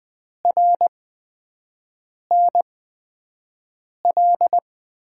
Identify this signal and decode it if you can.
Morse code, heard as a keyed tone, about 20 wpm, spaced as for 5 wpm, 700 Hz. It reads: RNL
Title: USA Call Signs - 6 Characters - 8x Character Spacing - 20wpm